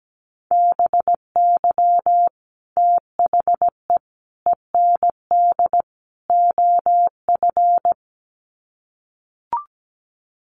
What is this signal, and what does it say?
Morse code, heard as a keyed tone, about 17 wpm, 700 Hz. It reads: BY THE END OF
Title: Sets of 4 Words 17wpm